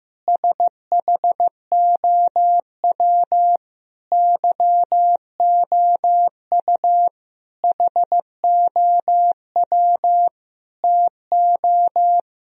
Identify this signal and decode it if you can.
Morse code, heard as a keyed tone, about 15 wpm, 700 Hz. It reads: SHOW YOU HOW TO
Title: Sentences from Top 1000 Words - Review Part 1 - 15wpm